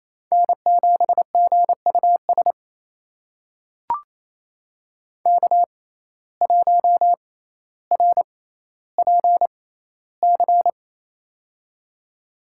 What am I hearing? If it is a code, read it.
Morse code, heard as a keyed tone, about 28 wpm, spaced as for 14 wpm, 700 Hz. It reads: N7GUS K1RPC
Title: USA Call Signs - 5 Characters - 6x Character Spacing - 28wpm